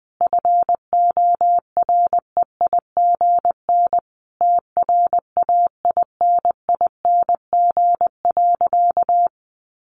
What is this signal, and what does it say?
Morse code, heard as a keyed tone, about 20 wpm, 700 Hz. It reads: FOREIGN TRAINING.
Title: Sentences from Top 1400 Words 20wpm